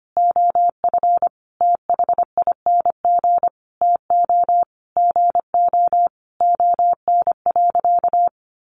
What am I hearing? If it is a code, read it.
Morse code, heard as a keyed tone, about 25 wpm, 700 Hz. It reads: OF THING TO GO ON.
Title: Sentences from Top 1300 Words 25wpm